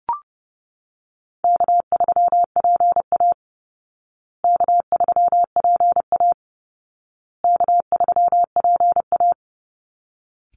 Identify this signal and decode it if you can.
Morse code, heard as a keyed tone, about 30 wpm, 700 Hz. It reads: K3PA K3PA K3PA
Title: Call Signs Repeated 2x - 30wpm